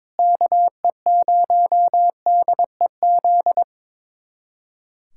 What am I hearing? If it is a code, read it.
Morse code, heard as a keyed tone, about 22 wpm, 700 Hz. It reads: KE0DEZ
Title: USA Callsign DE Callsign - 6 Characters - 22wpm